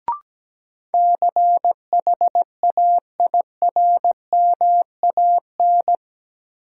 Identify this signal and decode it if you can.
Morse code, heard as a keyed tone, about 17 wpm, 700 Hz. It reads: CHAIRMAN
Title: Sets of 3 Words - New Word Drill - 17wpm